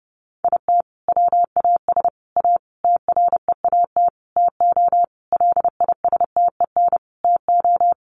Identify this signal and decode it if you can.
Morse code, heard as a keyed tone, about 30 wpm, 700 Hz. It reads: IT WAS A TREAT TO LISTEN TO
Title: Sentences from Top 1300 Words 30wpm